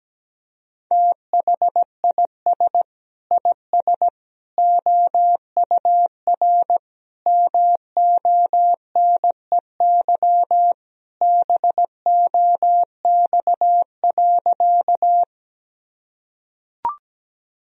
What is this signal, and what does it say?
Morse code, heard as a keyed tone, about 17 wpm, 700 Hz. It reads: THIS IS OUR MONEY BOX.
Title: Sentences from Top 1000 Words - Review Part 2 - 17wpm